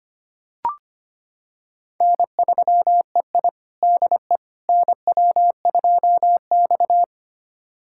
Morse code, heard as a keyed tone, about 25 wpm, 700 Hz. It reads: N3EI DE NW2X
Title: USA Callsign DE Callsign - 4 Characters - 25wpm